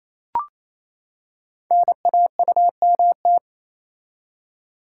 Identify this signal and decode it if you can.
Morse code, heard as a keyed tone, about 27 wpm, 700 Hz. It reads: NAUMT
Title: Code Groups - 5 Letters - 28wpm